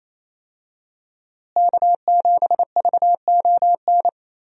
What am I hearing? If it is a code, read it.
Morse code, heard as a keyed tone, about 27 wpm, 700 Hz. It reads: K7VON
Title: USA Call Signs - 5 Characters - Repeated 2x - 28wpm